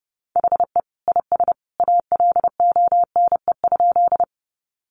Morse code, heard as a keyed tone, about 30 wpm, 700 Hz. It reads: HE IS ALONE?
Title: Sentences from Top 1000 Words 30wpm